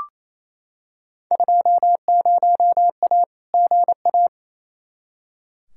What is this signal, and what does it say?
Morse code, heard as a keyed tone, about 28 wpm, 700 Hz. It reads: 20A GA
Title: Field Day Exchanges Encore - 28wpm